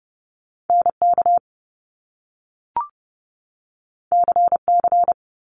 Morse code, heard as a keyed tone, about 30 wpm, 700 Hz. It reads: NK CC
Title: Top Two Letter Combinations in Words 30wpm